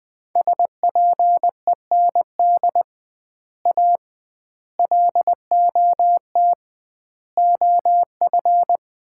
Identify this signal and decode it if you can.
Morse code, heard as a keyed tone, about 21 wpm, 700 Hz. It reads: SPEND A LOT OF
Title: 2x Word-Spacing - Sets of 5 Words - 20wpm